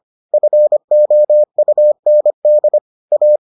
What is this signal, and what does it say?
Morse code, heard as a keyed tone, about 25 wpm, 600 Hz. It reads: FOUND A